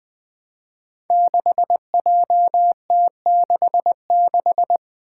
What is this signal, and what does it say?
Morse code, heard as a keyed tone, about 20 wpm, 700 Hz. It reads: 6JT66
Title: Five Challenging Letter-Number Combinations 20wpm